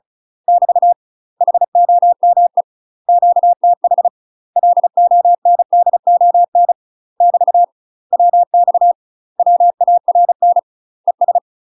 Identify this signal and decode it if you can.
Morse code, heard as a keyed tone, about 36 wpm, 700 Hz. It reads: <BT> HOME QTH LONDON <BT> WX WARN ES
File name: UK MASTER QSOs TRAINING / 35WPM / UK MASTER QSOs 0038